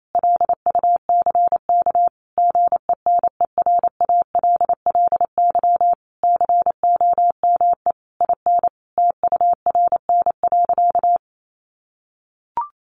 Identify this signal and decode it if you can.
Morse code, heard as a keyed tone, about 28 wpm, 700 Hz. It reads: LUCK GENERALLY COME IN TURN.
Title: Sentences from Top 1500 Words 28wpm